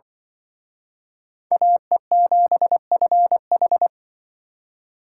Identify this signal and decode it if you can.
Morse code, heard as a keyed tone, about 25 wpm, 700 Hz. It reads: AE7FH
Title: Speed Racing - USA 2x2 Call Signs 20wpm